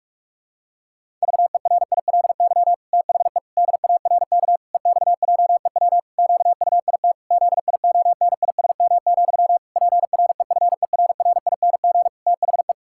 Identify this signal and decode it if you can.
Morse code, heard as a keyed tone, about 46 wpm, 700 Hz. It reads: VERILY THE DARK EXJEW QUIT ZIONISM, PREFERRING THE
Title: Pangram Challenge v2 - 45wpm